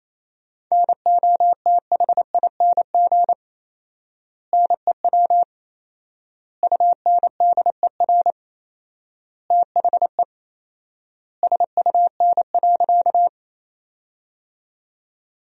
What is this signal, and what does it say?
Morse code, heard as a keyed tone, about 28 wpm, 700 Hz. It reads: NOTHING NEW UNDER THE SUN.
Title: 4x Word-Spacing - Sentences from Top 500 Words - 28wpm